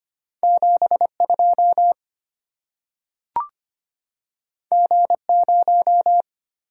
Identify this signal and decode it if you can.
Morse code, heard as a keyed tone, about 25 wpm, 700 Hz. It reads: 72 G0
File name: Two Letter-Number Combinations - 25wpm